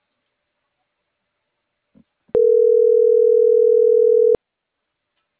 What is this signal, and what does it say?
Ringback tone (North America), 440 + 480 Hz.